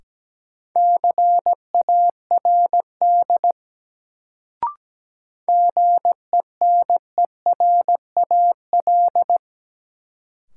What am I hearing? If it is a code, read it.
Morse code, heard as a keyed tone, about 17 wpm, 700 Hz. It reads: CARD GENERAL
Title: Sentences from Top 600 Words - New Word Drill - 17wpm